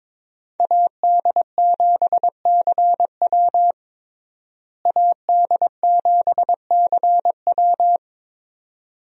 Morse code, heard as a keyed tone, about 22 wpm, 700 Hz. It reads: AD7CW AD7CW
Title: USA Call Signs - 5 Characters - Repeated 3x - 22wpm